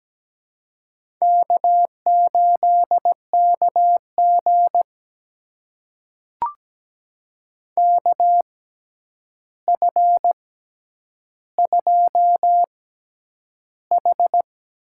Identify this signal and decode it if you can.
Morse code, heard as a keyed tone, about 17 wpm, spaced as for 9 wpm, 700 Hz. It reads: K8KG KF2H
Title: USA Call Signs - 4 Characters - 6x Character Spacing - 17wpm